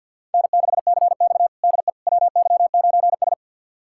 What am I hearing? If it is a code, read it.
Morse code, heard as a keyed tone, about 53 wpm, 700 Hz. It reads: N6CX DE WY9S
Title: USA Callsign DE Callsign - 4 Characters - 50wpm